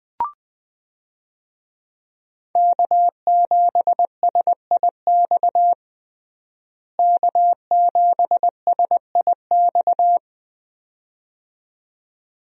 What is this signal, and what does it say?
Morse code, heard as a keyed tone, about 20 wpm, 700 Hz. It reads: K7SIX K7SIX
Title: USA Call Signs - 5 Characters - Repeated 2x - 20wpm